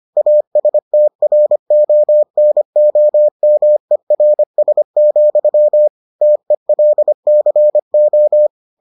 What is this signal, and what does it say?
Morse code, heard as a keyed tone, about 25 wpm, 600 Hz. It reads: ASTRONOMERS, TELCO